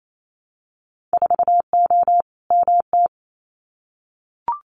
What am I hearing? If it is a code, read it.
Morse code, heard as a keyed tone, about 28 wpm, 690 Hz. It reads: 4O MT
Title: Winter Field Day Exchanges Encore 28wpm